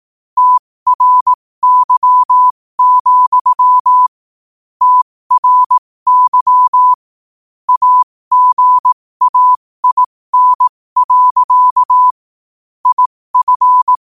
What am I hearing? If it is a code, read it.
Morse code, heard as a keyed tone, about 18 wpm, spaced as for 15 wpm, 1000 Hz. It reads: TRY, TRY AGAIN. IF
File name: qotdfiles / 15wpm